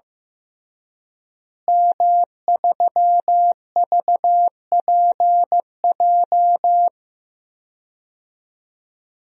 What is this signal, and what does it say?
Morse code, heard as a keyed tone, about 15 wpm, 700 Hz. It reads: M3VPJ
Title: DXCC-100 Call Signs - 5 Characters - Repeated 2x - 15wpm